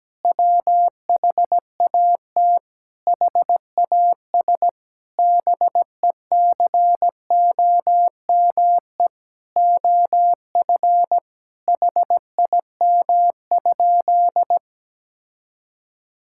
Morse code, heard as a keyed tone, about 17 wpm, 700 Hz. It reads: WHAT HAS BECOME OF HIM?
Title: Sentences from Top 1000 Words - Review Part 5 - 17wpm